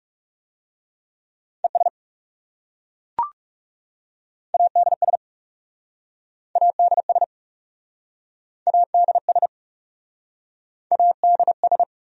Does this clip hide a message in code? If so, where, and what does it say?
Yes: Morse code from 1.6 to 11.8 s, heard as a keyed tone, about 40 wpm, 700 Hz. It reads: ES ADS ADS ADS ADS